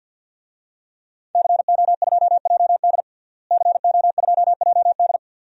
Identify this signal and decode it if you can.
Morse code, heard as a keyed tone, about 51 wpm, 700 Hz. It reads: CO2JD CO2JD
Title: DXCC-100 Call Signs - 5 Characters - Repeated 3x - 50wpm